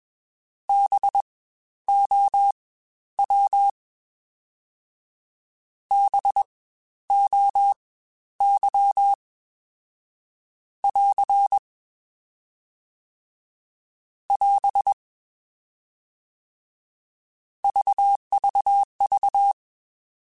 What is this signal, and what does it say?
Morse code, heard as a keyed tone, about 21 wpm, spaced as for 12 wpm, 800 Hz. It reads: BOW BOY <AR> <AS> VVV
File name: Spaced Alphas / Spaced Alphas mp3s / Bs sp @20